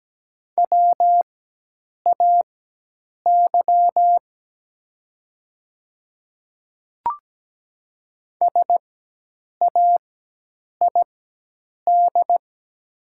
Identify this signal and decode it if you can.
Morse code, heard as a keyed tone, about 17 wpm, spaced as for 8 wpm, 700 Hz. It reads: WAY SAID
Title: 4x Character Spacing - Top 200 Words - 17wpm